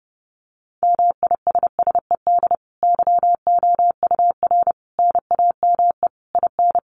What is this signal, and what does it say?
Morse code, heard as a keyed tone, about 30 wpm, 700 Hz. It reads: MISSED YOUR NAME IN
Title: Common QSO Phrases 30wpm